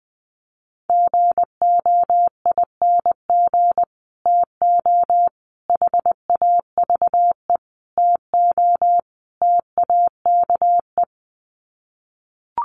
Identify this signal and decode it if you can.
Morse code, heard as a keyed tone, about 19 wpm, 700 Hz. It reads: GOING TO HAVE TO TAKE
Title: Sets of 5 Words Encore 20wpm